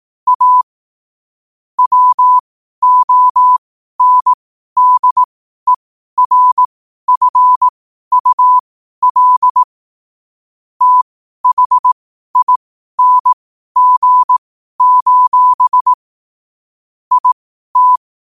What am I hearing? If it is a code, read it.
Morse code, heard as a keyed tone, about 18 wpm, spaced as for 12 wpm, 1000 Hz. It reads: A WONDERFUL THING: IT
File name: qotdfiles / 12wpm